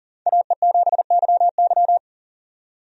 Morse code, heard as a keyed tone, about 40 wpm, 700 Hz. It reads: AE7YY
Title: USA Call Signs - 5 Characters - 40wpm